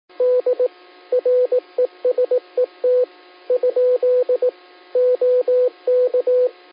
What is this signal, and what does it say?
Morse code, heard as a keyed tone, about 18 wpm, 500 Hz, with radio noise. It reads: D RESET ? OK
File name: qsorec SimplexLogic